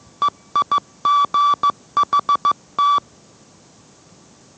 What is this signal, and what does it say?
Morse code, heard as a keyed tone, about 15 wpm, 1160 Hz. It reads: EIGHT